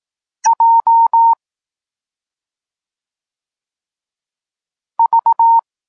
Morse code, heard as a keyed tone, about 18 wpm, 920 Hz. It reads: J V